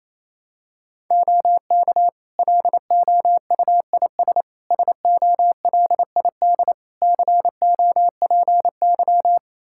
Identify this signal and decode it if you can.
Morse code, heard as a keyed tone, about 27 wpm, 700 Hz. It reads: OK LOUIS SOLID COPY
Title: Practice QSO - Exchange 3 - More Info - 28wpm